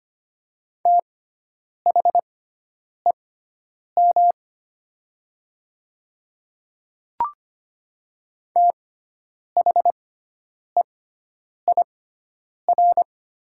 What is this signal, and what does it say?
Morse code, heard as a keyed tone, about 25 wpm, spaced as for 9 wpm, 700 Hz. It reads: THEM THEIR
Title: 6x Character Spacing - Top 100 Words - 25wpm